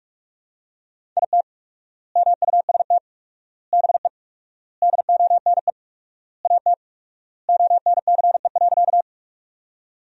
Morse code, heard as a keyed tone, about 46 wpm, 700 Hz. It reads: IT MUST BE DONE AT ONCE.
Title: 4x Word-Spacing - Sentences from Top 500 Words - 45wpm